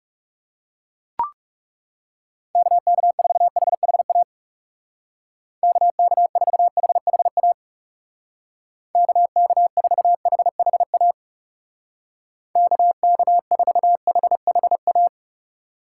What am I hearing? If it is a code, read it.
Morse code, heard as a keyed tone, about 36 wpm, 700 Hz. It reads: KK4HHA KK4HHA KK4HHA KK4HHA
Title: Speed Racing - USA 2x3 Call Signs 30wpm